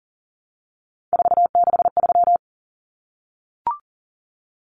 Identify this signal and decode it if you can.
Morse code, heard as a keyed tone, about 40 wpm, 700 Hz. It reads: 463
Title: Common Cut Numbers - 3-Digit 40wpm